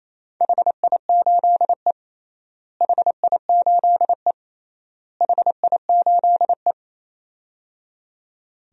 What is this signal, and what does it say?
Morse code, heard as a keyed tone, about 27 wpm, 700 Hz. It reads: HI8E HI8E HI8E
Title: DXCC-100 Call Signs - 4 Characters - Repeated 3x - 28wpm